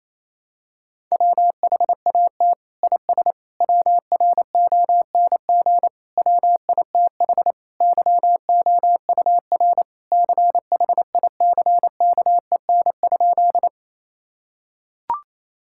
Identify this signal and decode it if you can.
Morse code, heard as a keyed tone, about 27 wpm, 700 Hz. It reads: WHAT IS WRONG WITH YOUR CHICKEN?
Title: Sentences from Top 1000 Words - Review Part 5 - 28wpm